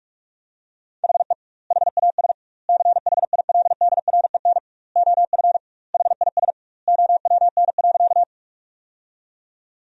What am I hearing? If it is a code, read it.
Morse code, heard as a keyed tone, about 44 wpm, 700 Hz. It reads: HE HAS CHILDREN OF HIS OWN.
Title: 2x Word-Spacing - Sentences from Top 300 Words - 45wpm